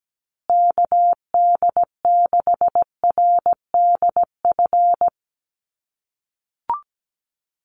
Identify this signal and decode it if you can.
Morse code, heard as a keyed tone, about 17 wpm, 700 Hz. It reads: KD6RDF
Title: USA Call Signs Repeated 3x 17wpm